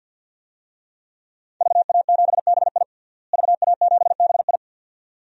Morse code, heard as a keyed tone, about 46 wpm, 700 Hz. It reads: VA7BI VA7BI
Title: DXCC-100 Call Signs - 5 Characters - Repeated 2x - 50wpm